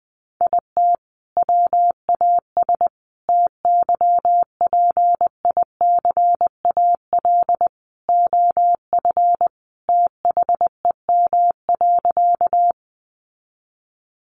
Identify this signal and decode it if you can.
Morse code, heard as a keyed tone, about 20 wpm, 700 Hz. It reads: IT WAS TYPICAL OF THEM.
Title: Sentences from Top 1100 Words 20wpm